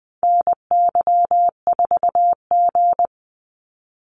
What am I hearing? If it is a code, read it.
Morse code, heard as a keyed tone, about 19 wpm, 700 Hz. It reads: NY4G
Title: NA SOTA Activator Callsigns 20wpm